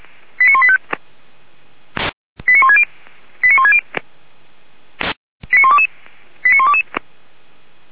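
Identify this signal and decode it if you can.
Five-tone selective call (ZVEI), heard as quick runs of five tones, tones of about 70 ms. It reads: 89185 89160 89160 89122 89122
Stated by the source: ZVEI five-tone call 89133 received via SDR, from Feuerwehr Orlamünde.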